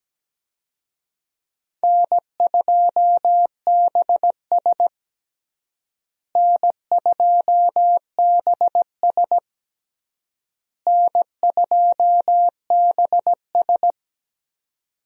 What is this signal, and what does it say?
Morse code, heard as a keyed tone, about 17 wpm, 700 Hz. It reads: N2BS N2BS N2BS
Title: USA Call Signs - 4 Characters - Repeated 3x - 17wpm